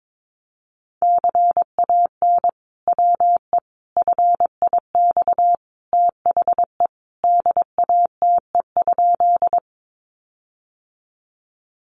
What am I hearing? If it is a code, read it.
Morse code, heard as a keyed tone, about 22 wpm, 700 Hz. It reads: CAN WE FIX THE DATE?
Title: Sentences from Top 1400 Words 22wpm